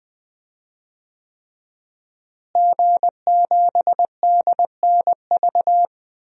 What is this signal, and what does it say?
Morse code, heard as a keyed tone, about 21 wpm, 700 Hz. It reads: G7DNV
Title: DXCC-100 Call Signs - 5 Characters - Repeated 3x - 20wpm